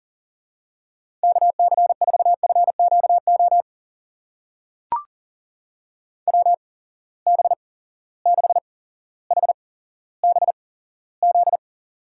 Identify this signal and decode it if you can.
Morse code, heard as a keyed tone, about 39 wpm, spaced as for 17 wpm, 700 Hz. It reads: KC4FQO WB6HBZ
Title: USA Call Signs - 6 Characters - 8x Character Spacing - 40wpm